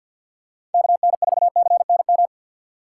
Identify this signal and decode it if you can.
Morse code, heard as a keyed tone, about 54 wpm, 700 Hz. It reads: KN4CNM
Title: USA Call Signs - 4x Character Spacing - 50wpm